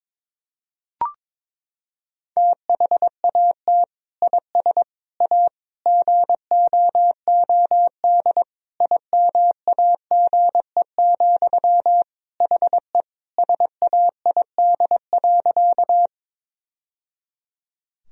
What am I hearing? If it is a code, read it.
Morse code, heard as a keyed tone, about 22 wpm, 700 Hz. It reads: THAT IS A GOOD IMAGE, HE SAID.
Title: Sentences from Top 1000 Words - Review Part 4 - 22wpm